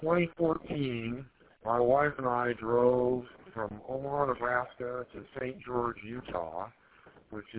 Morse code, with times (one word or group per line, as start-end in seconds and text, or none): none